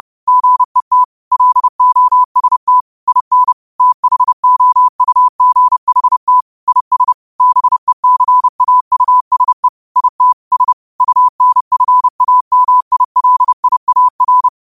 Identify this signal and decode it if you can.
Morse code, heard as a keyed tone, about 30 wpm, 1000 Hz. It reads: GET LOST IN THOUGHT IS BECAUSE IT S UNFAMILIAR
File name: qotdfiles / 30wpm